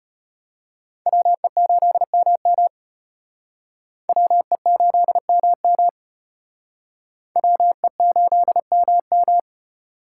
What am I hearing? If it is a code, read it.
Morse code, heard as a keyed tone, about 34 wpm, 700 Hz. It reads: WE8MM WE8MM WE8MM